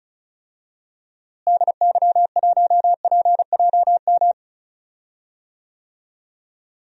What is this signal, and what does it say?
Morse code, heard as a keyed tone, about 36 wpm, 700 Hz. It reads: DY1PJM
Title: DXCC-100 Call Signs - 6 Characters - Repeated 3x - 35wpm